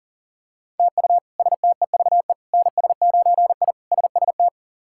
Morse code, heard as a keyed tone, about 40 wpm, 700 Hz. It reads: TU STEVE NS9I SST
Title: K1USN SST QSO Practice - 40wpm